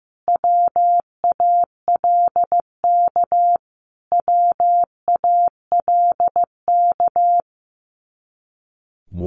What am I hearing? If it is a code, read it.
Morse code, heard as a keyed tone, about 15 wpm, 700 Hz. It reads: WALK WALK
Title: Top 300 Words Repeated 15wpm